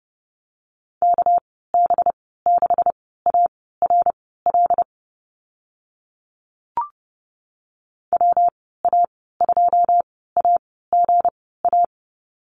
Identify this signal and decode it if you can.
Morse code, heard as a keyed tone, about 30 wpm, spaced as for 17 wpm, 700 Hz. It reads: KB6ARL WA2AGA
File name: Lesson 37g - Letter x - Call Signs 30wpm